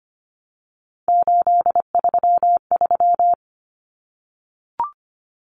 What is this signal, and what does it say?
Morse code, heard as a keyed tone, about 25 wpm, 700 Hz. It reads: 833
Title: Numbers - 3-Digit 25wpm